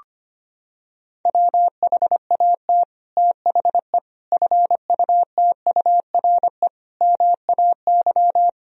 Morse code, heard as a keyed tone, about 25 wpm, 700 Hz. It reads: WHAT THE FUTURE MAY
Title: Sentences from Top 1000 Words - Review Part 2 - 25wpm